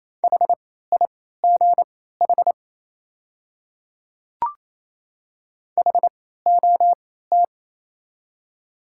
Morse code, heard as a keyed tone, about 27 wpm, spaced as for 16 wpm, 700 Hz. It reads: HIGH HOT
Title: 3x Character Spacing - Top 200 Words - 28wpm